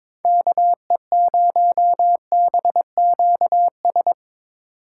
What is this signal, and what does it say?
Morse code, heard as a keyed tone, about 25 wpm, 700 Hz. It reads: KE0BQS
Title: Speed Racing - USA 2x3 Call Signs 22wpm